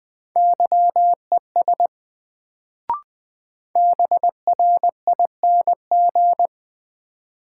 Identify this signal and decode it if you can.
Morse code, heard as a keyed tone, about 20 wpm, 700 Hz. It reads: YES BRING
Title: Sentences from Top 500 Words - New Word Drill - 20wpm